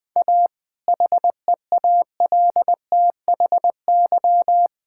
Morse code, heard as a keyed tone, about 20 wpm, 700 Hz. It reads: A HEALTHY
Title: Sentences from Top 1000 Words - Review Part 5 - 20wpm